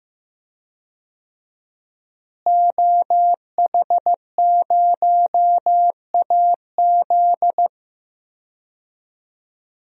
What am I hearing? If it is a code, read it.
Morse code, heard as a keyed tone, about 15 wpm, 700 Hz. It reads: OH0AZ
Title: DXCC-100 Call Signs - 5 Characters - Repeated 3x - 15wpm